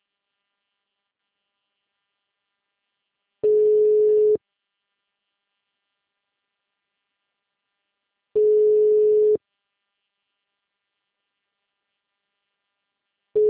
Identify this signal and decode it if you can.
Ringback tone (Europe, 425 Hz).
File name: out- -2020- - - .11222